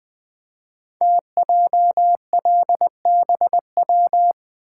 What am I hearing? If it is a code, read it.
Morse code, heard as a keyed tone, about 20 wpm, 700 Hz. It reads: TJLBW
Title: Code Groups - 5 Letter:Numbers - 20wpm